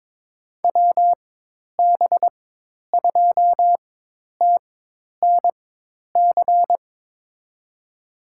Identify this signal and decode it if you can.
Morse code, heard as a keyed tone, about 22 wpm, spaced as for 10 wpm, 700 Hz. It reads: WB2TNC
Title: USA Call Signs - 4x Character Spacing - 22wpm